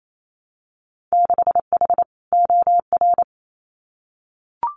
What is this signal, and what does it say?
Morse code, heard as a keyed tone, about 28 wpm, 700 Hz. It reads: 6H OR
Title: Winter Field Day Exchanges Encore 28wpm